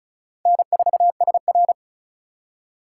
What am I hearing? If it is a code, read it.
Morse code, heard as a keyed tone, about 34 wpm, 700 Hz. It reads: N4SR
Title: K1USN SST QSO Practice - 35wpm